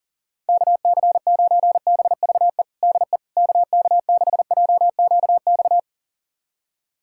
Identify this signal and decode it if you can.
Morse code, heard as a keyed tone, about 40 wpm, 700 Hz. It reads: KC9BVE DE KK6JQX
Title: USA Callsign DE Callsign - 6 Characters - 40wpm